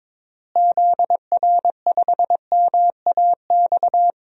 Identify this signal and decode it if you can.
Morse code, heard as a keyed tone, about 22 wpm, 700 Hz. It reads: ZR5MAX
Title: DXCC-100 Call Signs - 6 Characters - Repeated 3x - 22wpm